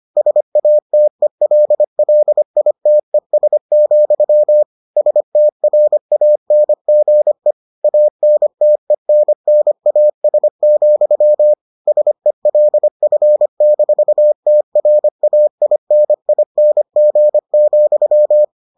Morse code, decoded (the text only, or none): SATELLITES, STRANGE ANTENNAS, SELF-TRAINING,